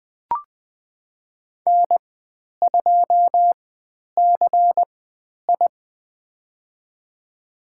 Morse code, heard as a keyed tone, about 20 wpm, spaced as for 10 wpm, 700 Hz. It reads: N2CI
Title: 109 Lesson 18g - Letter m - Call Signs 20wpm